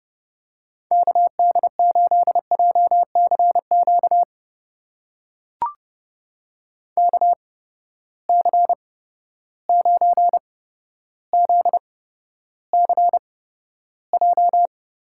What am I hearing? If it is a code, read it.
Morse code, heard as a keyed tone, about 32 wpm, spaced as for 13 wpm, 700 Hz. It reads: KD8JCQ KC9ZCJ
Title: USA Call Signs - 6 Characters - 8x Character Spacing - 30wpm